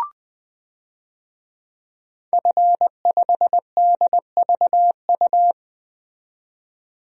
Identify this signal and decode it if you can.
Morse code, heard as a keyed tone, about 21 wpm, 700 Hz. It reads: F5DVU
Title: DXCC-100 Call Signs - 5 Characters - 20wpm